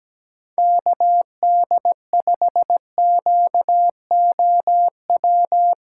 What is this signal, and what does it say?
Morse code, heard as a keyed tone, about 17 wpm, 700 Hz. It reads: KD5QOW
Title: USA Callsign DE Callsign 17wpm